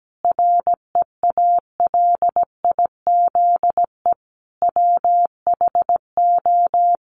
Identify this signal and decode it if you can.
Morse code, heard as a keyed tone, about 17 wpm, 700 Hz. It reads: REALIZE WHO
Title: Sentences from Top 1000 Words 17wpm